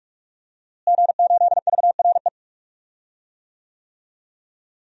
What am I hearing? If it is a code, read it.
Morse code, heard as a keyed tone, about 45 wpm, 700 Hz. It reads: G8VRE
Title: DXCC-100 Call Signs - 5 Characters - Repeated 3x - 45wpm